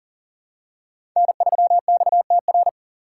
Morse code, heard as a keyed tone, about 44 wpm, 690 Hz. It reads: N3XTR
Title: USA Call Signs - 5 Characters - 6x Character Spacing - 40wpm